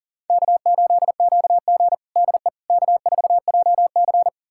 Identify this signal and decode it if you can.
Morse code, heard as a keyed tone, about 40 wpm, 700 Hz. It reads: K8QG DE K4JC
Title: USA Callsign DE Callsign - 4 Characters - 40wpm